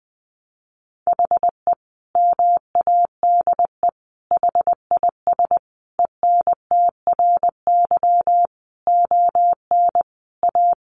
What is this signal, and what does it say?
Morse code, heard as a keyed tone, about 20 wpm, 700 Hz. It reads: HE MADE HIS ENTRY ON A
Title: Sentences from Top 1500 Words 20wpm